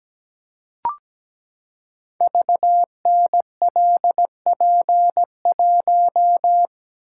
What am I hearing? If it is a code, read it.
Morse code, heard as a keyed tone, about 17 wpm, 700 Hz. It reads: VNLP1
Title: Code Groups - 5 Letter:Numbers - 17wpm